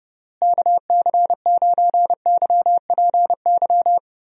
Morse code, heard as a keyed tone, about 30 wpm, 700 Hz. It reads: KC9YPY
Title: USA Call Signs - 6 Characters - Repeated 2x - 30wpm